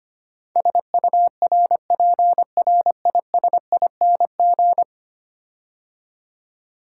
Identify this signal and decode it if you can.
Morse code, heard as a keyed tone, about 25 wpm, 700 Hz. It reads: SURPRISING
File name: 10 Letter Words - First 3 Letters Given - 25wpm